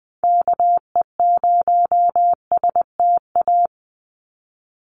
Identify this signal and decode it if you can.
Morse code, heard as a keyed tone, about 20 wpm, 700 Hz. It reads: KE0STA
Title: USA Call Signs 20wpm